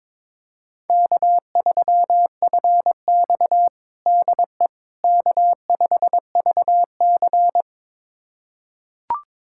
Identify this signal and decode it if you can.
Morse code, heard as a keyed tone, about 22 wpm, 700 Hz. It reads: K3FX DE K5VC
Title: USA Callsign DE Callsign - 4 Characters - 22wpm